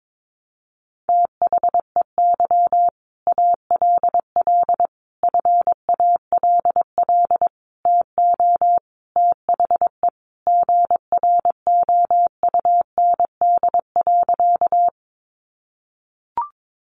Morse code, heard as a keyed tone, about 22 wpm, 700 Hz. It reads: THEY ALL FALL TO THE GROUND.
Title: Sentences from Top 500 Words 22wpm